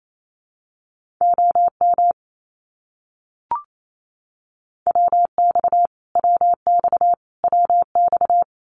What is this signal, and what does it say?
Morse code, heard as a keyed tone, about 28 wpm, 700 Hz. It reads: OM WX WX WX
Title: Abbreviations Repeated 3x - 28wpm